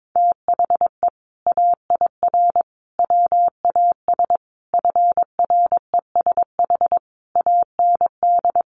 Morse code, heard as a keyed tone, about 22 wpm, 700 Hz. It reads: THE AIR WAS FRESH AND
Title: Sentences from Top 1200 Words 22wpm